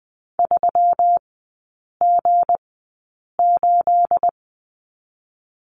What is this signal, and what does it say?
Morse code, heard as a keyed tone, about 20 wpm, spaced as for 8 wpm, 700 Hz. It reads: 3G8
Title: 218 Lesson 35e - Number 0 - Three Characters 20wpm